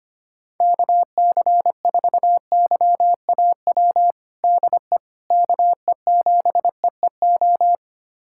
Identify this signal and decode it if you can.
Morse code, heard as a keyed tone, about 25 wpm, 700 Hz. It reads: KC4YAW DE KE7EEO
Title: USA Callsign DE Callsign 25wpm